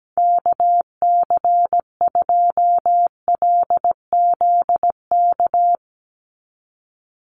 Morse code, heard as a keyed tone, about 17 wpm, 700 Hz. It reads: KC2LZK
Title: USA Call Signs 17wpm